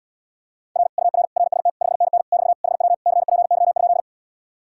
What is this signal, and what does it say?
Morse code, heard as a keyed tone, about 80 wpm, 700 Hz. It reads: R UR RST 5NN <BT> HW CPY?